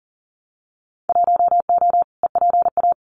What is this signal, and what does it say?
Morse code, heard as a keyed tone, about 38 wpm, 700 Hz. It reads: 1O EPA
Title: Winter Field Day Exchanges 40wpm